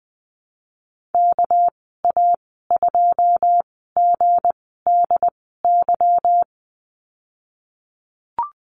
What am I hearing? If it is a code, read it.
Morse code, heard as a keyed tone, about 19 wpm, spaced as for 14 wpm, 700 Hz. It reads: KA2GDY
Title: 238 Lesson 38g - Letter q - Call Signs 20wpm